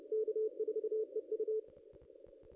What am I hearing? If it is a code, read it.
Morse code, heard as a keyed tone, about 25 wpm, 440 Hz. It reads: K4EU
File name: contests / ss-cw-k5tr-op-n5rz -2200